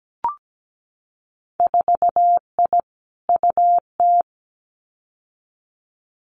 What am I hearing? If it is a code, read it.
Morse code, heard as a keyed tone, about 17 wpm, 700 Hz. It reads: 4I UT
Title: Winter Field Day Exchanges 17wpm